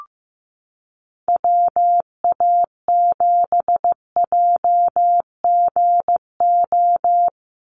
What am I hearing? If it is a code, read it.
Morse code, heard as a keyed tone, about 15 wpm, 700 Hz. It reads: WA7JGO
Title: USA Call Signs 15wpm